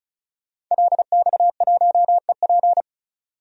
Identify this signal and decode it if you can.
Morse code, heard as a keyed tone, about 34 wpm, 700 Hz. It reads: LX1EP
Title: DXCC-100 Call Signs - 5 Characters - 35wpm